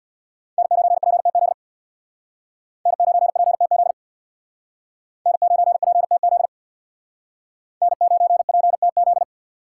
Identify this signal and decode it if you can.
Morse code, heard as a keyed tone, about 62 wpm, 700 Hz. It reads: N9PTZ N9PTZ N9PTZ N9PTZ
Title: Speed Racing - USA 1x3 Call Signs 50wpm